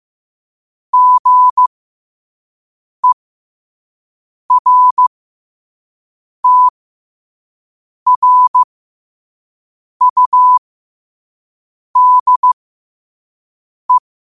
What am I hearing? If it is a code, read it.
Morse code, heard as a keyed tone, about 15 wpm, spaced as for 5 wpm, 1000 Hz. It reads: GERTRUDE
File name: qotdfiles / 5wpm